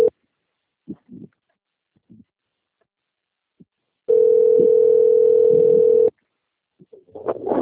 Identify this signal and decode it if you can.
Ringback tone (North America), 440 + 480 Hz.
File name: out- - - - - .961